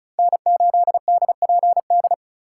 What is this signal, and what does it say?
Morse code, heard as a keyed tone, about 36 wpm, 700 Hz. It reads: N8DPD
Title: K1USN SST QSO Practice - 35wpm